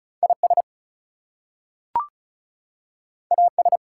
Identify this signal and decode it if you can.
Morse code, heard as a keyed tone, about 36 wpm, 700 Hz. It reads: IS AS
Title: Two Letter Words - 35wpm